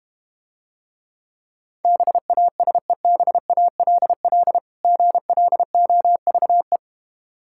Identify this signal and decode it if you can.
Morse code, heard as a keyed tone, about 32 wpm, 700 Hz. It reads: BASEBALL GLOVE